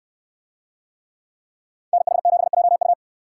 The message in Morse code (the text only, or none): NH7JU